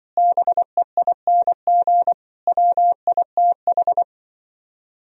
Morse code, heard as a keyed tone, about 24 wpm, 700 Hz. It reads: BEING WITH
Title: Speed-Racing - Sets of 5 Words 20wpm